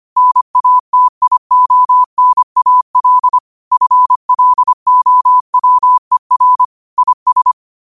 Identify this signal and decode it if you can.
Morse code, heard as a keyed tone, about 25 wpm, 1000 Hz. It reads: NATIONAL FLOWER IS
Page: New quotes every day in morse code at 25 Words per minute.